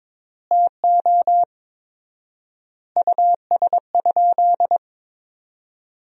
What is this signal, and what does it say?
Morse code, heard as a keyed tone, about 22 wpm, 690 Hz. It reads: TO US?
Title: 4x Word-Spacing - Sentences from Top 400 Words - 22wpm